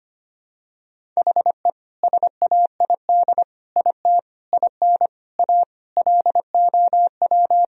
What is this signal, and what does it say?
Morse code, heard as a keyed tone, about 25 wpm, 700 Hz. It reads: HE SAID IT IN A LOW
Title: Sentences from Top 1000 Words - Review Part 1 - 25wpm